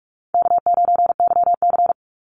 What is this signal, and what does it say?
Morse code, heard as a keyed tone, about 46 wpm, 700 Hz. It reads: K9YC
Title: Easy Call Signs - 45wpm